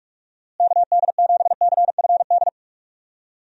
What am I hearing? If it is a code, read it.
Morse code, heard as a keyed tone, about 46 wpm, 700 Hz. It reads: KD7CFD
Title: USA Call Signs - 6 Characters - Repeated 3x - 45wpm